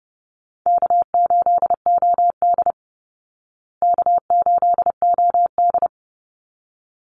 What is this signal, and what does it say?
Morse code, heard as a keyed tone, about 30 wpm, 700 Hz. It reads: K8OD K8OD
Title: Easy Call Signs Repeated 2x - 30wpm